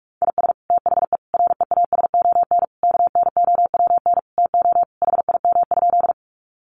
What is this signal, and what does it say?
Morse code, heard as a keyed tone, about 44 wpm, 700 Hz. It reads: IS THE REASON KNOWN TO HIM?
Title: Sentences from Top 700 Words 45wpm